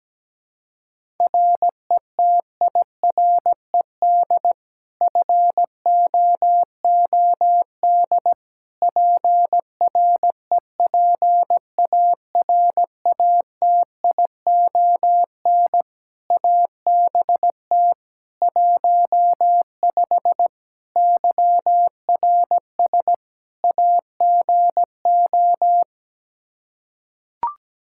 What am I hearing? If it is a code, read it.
Morse code, heard as a keyed tone, about 17 wpm, 700 Hz. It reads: RETIRED FOOD PREPARATION ABT 15 YRS AGO
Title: K1USN SST QSO Practice 17wpm